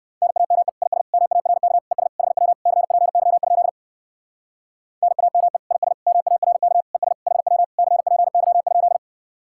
Morse code, heard as a keyed tone, about 58 wpm, 700 Hz. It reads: NAME IS MARK ES HW CPY? NAME IS MARK ES HW CPY?